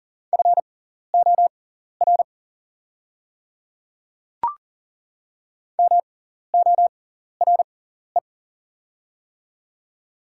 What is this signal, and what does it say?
Morse code, heard as a keyed tone, about 41 wpm, spaced as for 14 wpm, 690 Hz. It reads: FOR MORE
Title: 6x Character Spacing - Top 300 Words - 40wpm